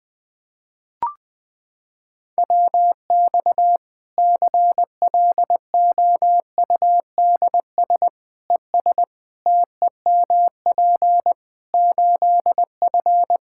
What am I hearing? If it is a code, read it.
Morse code, heard as a keyed tone, about 20 wpm, 700 Hz. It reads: WX CLOUDS ES TEMP 8F
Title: Practice QSO - Exchange 2 - Info - 20wpm